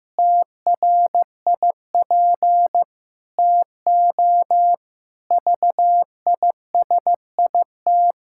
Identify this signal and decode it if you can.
Morse code, heard as a keyed tone, about 15 wpm, 700 Hz. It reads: TRIP TO VISIT
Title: Pangram Challenge v2 - 15wpm